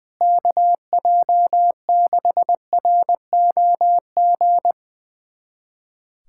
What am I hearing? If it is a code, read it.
Morse code, heard as a keyed tone, about 20 wpm, 700 Hz. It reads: KJ6ROG
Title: USA Callsign DE Callsign - 6 Characters - 20wpm